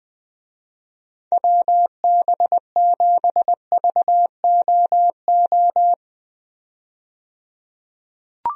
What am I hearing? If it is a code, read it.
Morse code, heard as a keyed tone, about 20 wpm, 700 Hz. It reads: WB7VOO
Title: USA Call Signs - 6 Characters - 20wpm